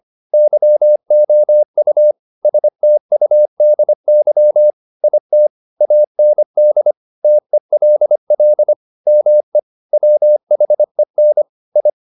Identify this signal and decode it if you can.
Morse code, heard as a keyed tone, about 25 wpm, 600 Hz. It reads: YOU STUDY IT AND TELL ME WHEN I
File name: the old man and the sea part 1 25wpm